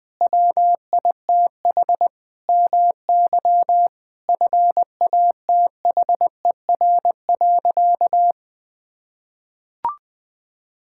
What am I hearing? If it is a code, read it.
Morse code, heard as a keyed tone, about 20 wpm, 700 Hz. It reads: WITH MY FATHER.
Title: Sentences from Top 1000 Words - Review Part 5 - 20wpm